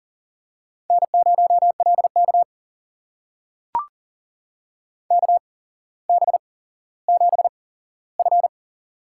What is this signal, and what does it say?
Morse code, heard as a keyed tone, about 38 wpm, spaced as for 17 wpm, 700 Hz. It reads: N0LK KB7F
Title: USA Call Signs - 4 Characters - 8x Character Spacing - 40wpm